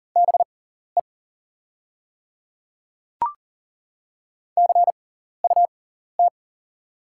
Morse code, heard as a keyed tone, about 38 wpm, spaced as for 14 wpm, 700 Hz. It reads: BE CUT
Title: 6x Character Spacing - Top 500 Words - 40wpm